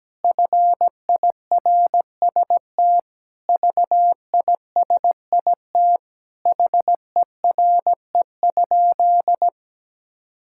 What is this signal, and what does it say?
Morse code, heard as a keyed tone, about 17 wpm, 700 Hz. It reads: FIRST VISIT HERE?
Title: Sentences from Top 1000 Words - Review Part 6 - 17wpm